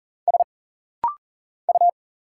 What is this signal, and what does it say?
Morse code, heard as a keyed tone, about 39 wpm, 700 Hz. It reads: S U
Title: Single Letter-Number Alt - Rapid-Fire 40wpm